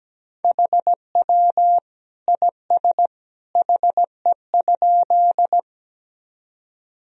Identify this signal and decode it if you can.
Morse code, heard as a keyed tone, about 17 wpm, 700 Hz. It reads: HW IS HE?
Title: Sentences from Top 100 Words and Abbreviations 17wpm